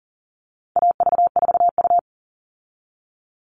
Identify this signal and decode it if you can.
Morse code, heard as a keyed tone, about 40 wpm, 700 Hz. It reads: AV4U
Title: All Cut Numbers - 4-Digit 40wpm